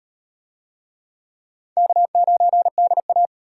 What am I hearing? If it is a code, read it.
Morse code, heard as a keyed tone, about 34 wpm, 700 Hz. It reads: K9DA